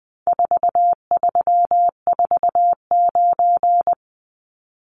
Morse code, heard as a keyed tone, about 20 wpm, 700 Hz. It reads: 4349
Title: Numbers - 4-Digit 20wpm